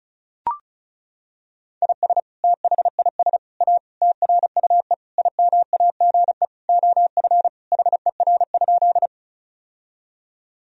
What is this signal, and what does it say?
Morse code, heard as a keyed tone, about 36 wpm, 700 Hz. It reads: IS THIS A TRUE IMAGE OF HER?
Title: Sentences from Top 1000 Words - Review Part 4 - 35wpm